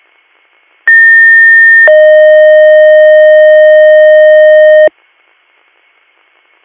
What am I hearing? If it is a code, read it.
Two-tone page, heard as a one-second tone, then a three-second tone, about 1800 then 620 Hz. It is heard from 0.9 to 4.9 s.